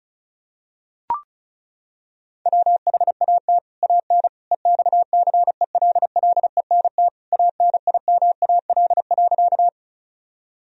Morse code, heard as a keyed tone, about 34 wpm, 700 Hz. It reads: WHAT AN EXCELLENT ANIMAL.
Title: Sentences from Top 1000 Words - Review Part 1 - 35wpm